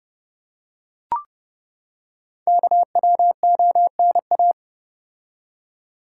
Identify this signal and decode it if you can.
Morse code, heard as a keyed tone, about 30 wpm, 700 Hz. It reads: KWONA
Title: Code Groups - 5 Letters - 30wpm